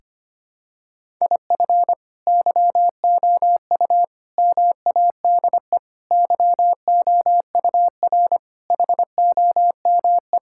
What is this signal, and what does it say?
Morse code, heard as a keyed tone, about 25 wpm, 700 Hz. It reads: IF YOU MADE YOUR HOME
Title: Sentences from Top 1600 Words 25wpm